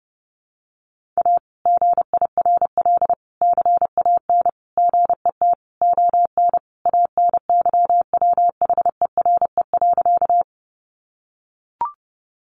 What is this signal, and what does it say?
Morse code, heard as a keyed tone, about 30 wpm, 700 Hz. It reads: A GIRL CAN GET ON ANYWHERE.
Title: Sentences from Top 1400 Words 30wpm